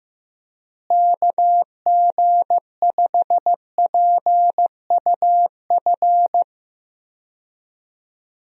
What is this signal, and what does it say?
Morse code, heard as a keyed tone, about 15 wpm, 700 Hz. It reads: KG5PUF
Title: USA Call Signs - 6 Characters - Repeated 2x - 15wpm